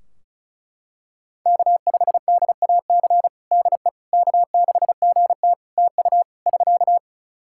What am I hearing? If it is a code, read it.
Morse code, heard as a keyed tone, about 36 wpm, 700 Hz. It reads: K5DAC DE K6GT TU <SK>
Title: Practice QSO - Exchange 4 - Ending - 35wpm